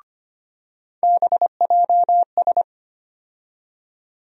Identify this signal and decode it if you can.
Morse code, heard as a keyed tone, about 25 wpm, 700 Hz. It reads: BJS
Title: USA Call Sign Postfixes - 3 Letters 25wpm